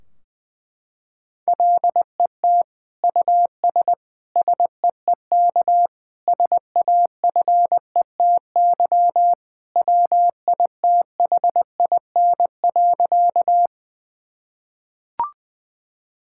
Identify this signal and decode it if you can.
Morse code, heard as a keyed tone, about 21 wpm, 700 Hz. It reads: LET US SEEK SAFETY WITHIN.
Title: Sentences from Top 1600 Words 20wpm